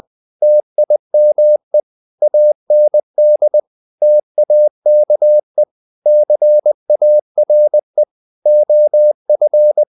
Morse code, heard as a keyed tone, about 20 wpm, 600 Hz. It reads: TIME AND TAKE CARE OF
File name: the old man and the sea / the old man and the sea part 1 20wpm